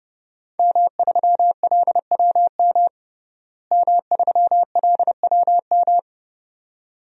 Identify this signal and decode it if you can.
Morse code, heard as a keyed tone, about 32 wpm, 700 Hz. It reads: M3LWM M3LWM
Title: DXCC-100 Call Signs - 5 Characters - Repeated 2x - 30wpm